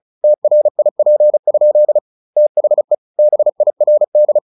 Morse code, heard as a keyed tone, about 34 wpm, 600 Hz. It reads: TRIP? THE BIRD
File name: the old man and the sea / the old man and the sea 1 35wpm